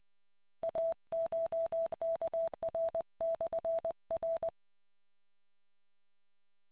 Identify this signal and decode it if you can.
Morse code, heard as a keyed tone, about 20 wpm, 660 Hz. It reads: ACHKR/R